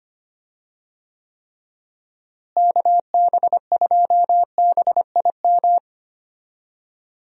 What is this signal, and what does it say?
Morse code, heard as a keyed tone, about 25 wpm, 700 Hz. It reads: KB2BIM
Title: USA Call Signs - 6 Characters - Repeated 2x - 25wpm